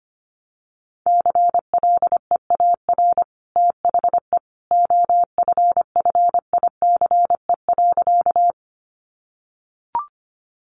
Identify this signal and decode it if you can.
Morse code, heard as a keyed tone, about 25 wpm, 700 Hz. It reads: CLEAR THE OFFICE.
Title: Sentences from Top 1000 Words 25wpm